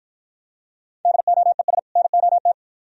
Morse code, heard as a keyed tone, about 53 wpm, 690 Hz. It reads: DOES NOT